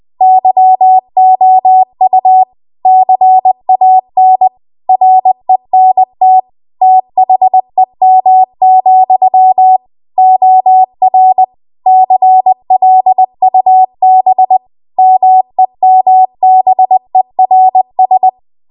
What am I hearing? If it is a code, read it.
Morse code, heard as a keyed tone, about 20 wpm, 750 Hz. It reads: YOU CAN RENT THEM, OR CLUB MEMBERS